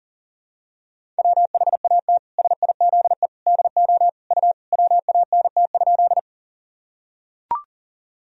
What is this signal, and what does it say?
Morse code, heard as a keyed tone, about 40 wpm, 700 Hz. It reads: WHAT SIZE DO U WANT?
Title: Sentences from Top 400 Words and Abbreviations 40wpm